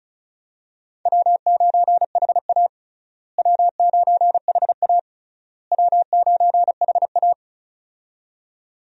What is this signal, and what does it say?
Morse code, heard as a keyed tone, about 34 wpm, 700 Hz. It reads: W9HA W9HA W9HA
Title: USA Call Signs - 4 Characters - Repeated 3x - 35wpm